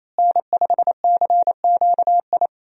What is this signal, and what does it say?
Morse code, heard as a keyed tone, about 28 wpm, 700 Hz. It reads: N5CQI
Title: USA Call Signs - 5 Characters - Repeated 3x - 28wpm